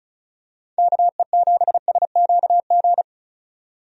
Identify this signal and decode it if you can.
Morse code, heard as a keyed tone, about 36 wpm, 700 Hz. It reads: KE7SQG
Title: USA Call Signs - 6 Characters - 35wpm